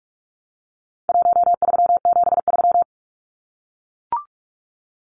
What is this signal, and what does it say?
Morse code, heard as a keyed tone, about 44 wpm, 700 Hz. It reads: 1373
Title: Numbers - Contest Style - 4-Digit 45wpm